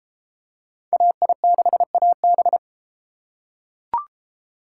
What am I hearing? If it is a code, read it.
Morse code, heard as a keyed tone, about 31 wpm, 700 Hz. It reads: AI6AB